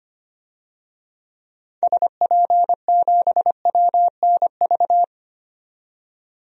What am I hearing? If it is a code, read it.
Morse code, heard as a keyed tone, about 25 wpm, 700 Hz. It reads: SP7WNV
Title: DXCC-100 Call Signs - 6 Characters - 25wpm